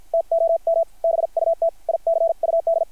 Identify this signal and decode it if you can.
Morse code, heard as a keyed tone, about 53 wpm, spaced as for 45 wpm, 660 Hz. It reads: TOM BUT I CUD
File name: archived / CRSnet